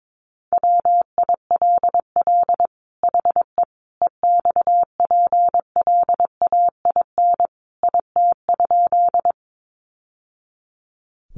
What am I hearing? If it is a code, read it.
Morse code, heard as a keyed tone, about 22 wpm, 700 Hz. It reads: WILL HE EXPLAIN IT?
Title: Sentences from Top 1100 Words 22wpm